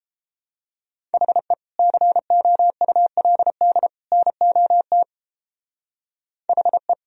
Morse code, heard as a keyed tone, about 29 wpm, 700 Hz. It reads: HE COULD NOT HE